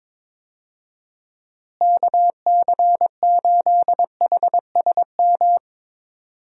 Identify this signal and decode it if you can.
Morse code, heard as a keyed tone, about 22 wpm, 700 Hz. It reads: KC8HSM
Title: USA Call Signs - 6 Characters - Repeated 2x - 22wpm